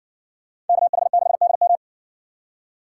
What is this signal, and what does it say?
Morse code, heard as a keyed tone, about 60 wpm, 700 Hz. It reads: KH6DM